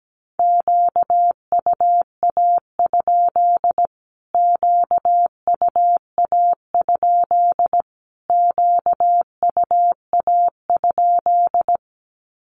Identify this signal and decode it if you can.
Morse code, heard as a keyed tone, about 17 wpm, 700 Hz. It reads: QUA? QUA? QUA?
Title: Q-Codes Repeated 3x 17wpm